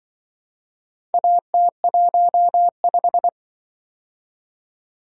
Morse code, heard as a keyed tone, about 25 wpm, 700 Hz. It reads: AT15